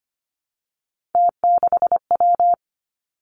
Morse code, heard as a keyed tone, about 25 wpm, 700 Hz. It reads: T6W
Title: Three Challenging Letter-Number Combinations 25wpm